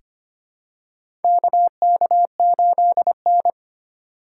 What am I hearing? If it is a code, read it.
Morse code, heard as a keyed tone, about 25 wpm, 700 Hz. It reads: KK8N
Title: USA Call Signs - 4 Characters - 8x Character Spacing - 25wpm